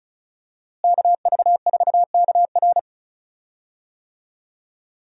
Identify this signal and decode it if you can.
Morse code, heard as a keyed tone, about 36 wpm, 700 Hz. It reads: KV4KR
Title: USA Call Signs - 5 Characters - Repeated 3x - 35wpm